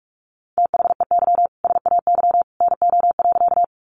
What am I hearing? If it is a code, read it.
Morse code, heard as a keyed tone, about 44 wpm, 700 Hz. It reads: THEY SAY NO.
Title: Easy Phrases 45wpm